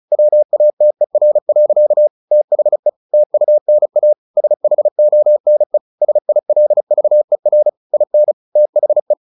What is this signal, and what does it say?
Morse code, heard as a keyed tone, about 36 wpm, 600 Hz. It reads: WATER. THE TUNA SHONE SILVER IN THE
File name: the old man and the sea / the old man and the sea 1 35wpm